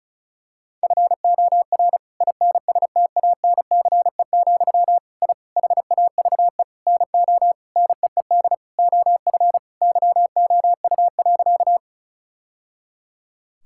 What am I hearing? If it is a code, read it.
Morse code, heard as a keyed tone, about 36 wpm, 700 Hz. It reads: FOR INSTANCE, I HAVE NO NEED OF YOU.
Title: Sentences from Top 1000 Words - Review Part 5 - 35wpm